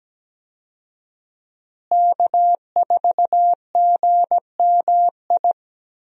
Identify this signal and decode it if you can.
Morse code, heard as a keyed tone, about 17 wpm, 700 Hz. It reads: K4GMI
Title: USA Call Signs - 5 Characters - Repeated 2x - 17wpm